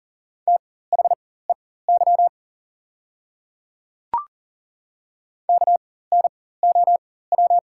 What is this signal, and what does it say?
Morse code, heard as a keyed tone, about 38 wpm, spaced as for 18 wpm, 690 Hz. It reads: THEY KNOW
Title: 4x Character Spacing - Top 300 Words - 40wpm